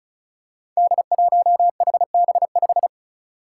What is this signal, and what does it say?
Morse code, heard as a keyed tone, about 36 wpm, 700 Hz. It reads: D1HB5
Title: Five Challenging Letter-Number Combinations 35wpm